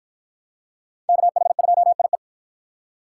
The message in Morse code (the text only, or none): KH2IE